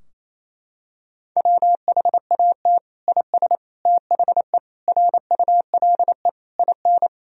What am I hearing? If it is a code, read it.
Morse code, heard as a keyed tone, about 27 wpm, 700 Hz. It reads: WHAT IS THE RULE IN
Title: Sentences from Top 1000 Words - Review Part 2 - 28wpm